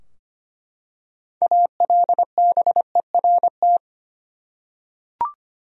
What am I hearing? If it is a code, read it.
Morse code, heard as a keyed tone, about 25 wpm, 700 Hz. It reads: ALBERT
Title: USA Common Names - 25wpm